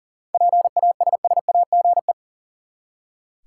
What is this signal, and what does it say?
Morse code, heard as a keyed tone, about 40 wpm, 700 Hz. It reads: PASSAGE
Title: Seven Letter Words - 40wpm